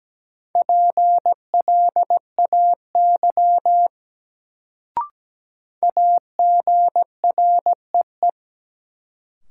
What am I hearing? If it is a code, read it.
Morse code, heard as a keyed tone, about 17 wpm, 700 Hz. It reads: PLAY AGREE
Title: Sets of 5 Words Encore - New Word Drill - 17wpm